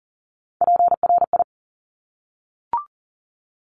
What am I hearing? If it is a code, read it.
Morse code, heard as a keyed tone, about 42 wpm, 700 Hz. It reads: PRI
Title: Top Three Letter Combinations in Words 40wpm